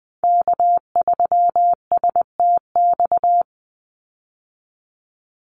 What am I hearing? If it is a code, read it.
Morse code, heard as a keyed tone, about 20 wpm, 700 Hz. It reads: K3STX
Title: Call Signs Repeated 3x - 20wpm